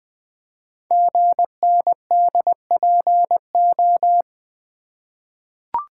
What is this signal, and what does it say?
Morse code, heard as a keyed tone, about 20 wpm, 700 Hz. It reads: GNDPO
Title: Code Groups - 5 Letters - 20wpm